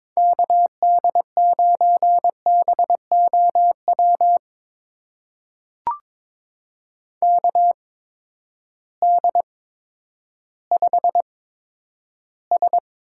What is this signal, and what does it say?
Morse code, heard as a keyed tone, about 22 wpm, spaced as for 11 wpm, 700 Hz. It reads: KD9BOW KD5S
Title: USA Call Signs - 6 Characters - 8x Character Spacing - 22wpm